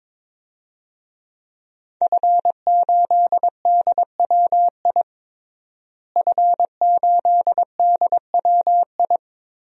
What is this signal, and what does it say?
Morse code, heard as a keyed tone, about 22 wpm, 700 Hz. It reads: F8DWI F8DWI
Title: DXCC-100 Call Signs - 5 Characters - Repeated 3x - 22wpm